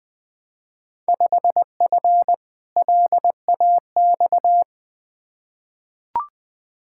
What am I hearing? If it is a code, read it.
Morse code, heard as a keyed tone, about 20 wpm, 700 Hz. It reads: 5F LAX
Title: Field Day Exchanges Encore - 20wpm